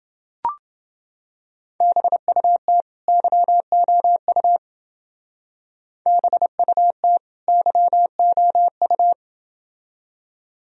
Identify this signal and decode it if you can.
Morse code, heard as a keyed tone, about 29 wpm, 700 Hz. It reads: BUT YOU BUT YOU